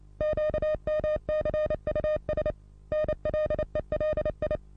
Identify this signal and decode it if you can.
Morse code, heard as a keyed tone, about 28 wpm, 610 Hz. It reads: QMCUS NLELI
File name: repos / Morsekurs / DJ4UF Morsen27